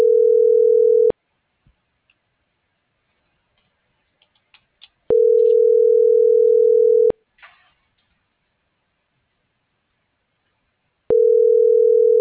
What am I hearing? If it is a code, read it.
Ringback tone (North America), 440 + 480 Hz.